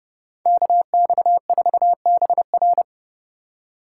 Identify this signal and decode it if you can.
Morse code, heard as a keyed tone, about 39 wpm, 700 Hz. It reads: KX4BR